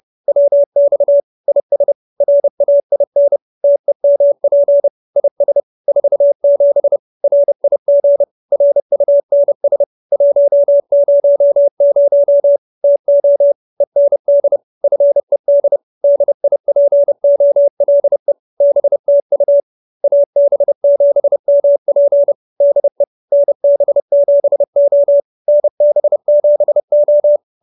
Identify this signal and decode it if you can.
Morse code, heard as a keyed tone, about 30 wpm, 570 Hz. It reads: WX IS RAIN TEMP IS 47 RIG RUNS 100 TO END FED DIPOLE BTU AB7MP DE NB7O NB7O